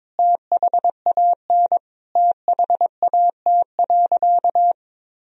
Morse code, heard as a keyed tone, about 22 wpm, 700 Hz. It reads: THAN THAT.
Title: Sentences from Top 1000 Words - Review Part 2 - 22wpm